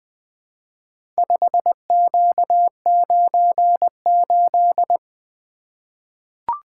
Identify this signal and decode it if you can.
Morse code, heard as a keyed tone, about 20 wpm, 700 Hz. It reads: 5Q98
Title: Four Letter-Number Combinations - 20wpm